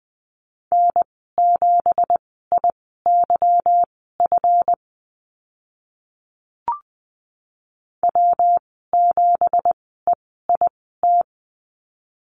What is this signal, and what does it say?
Morse code, heard as a keyed tone, about 20 wpm, spaced as for 14 wpm, 700 Hz. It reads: N7IYF W7EIT
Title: 167 Lesson 27g - Number 7 - Call Signs 20wpm